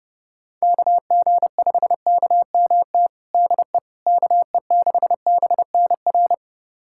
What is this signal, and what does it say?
Morse code, heard as a keyed tone, about 32 wpm, 700 Hz. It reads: KG5KMT DE KE6BNR
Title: USA Callsign DE Callsign - 6 Characters - 30wpm